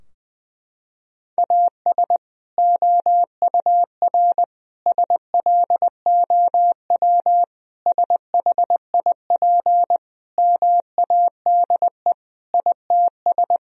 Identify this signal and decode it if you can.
Morse code, heard as a keyed tone, about 20 wpm, 700 Hz. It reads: AS OUR SLOW SHIP MADE ITS
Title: Sentences from Top 1000 Words - Review Part 2 - 20wpm